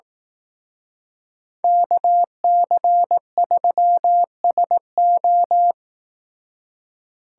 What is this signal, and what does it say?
Morse code, heard as a keyed tone, about 18 wpm, 700 Hz. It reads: KC3SO
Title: Speed Racing - USA 2x2 Call Signs 15wpm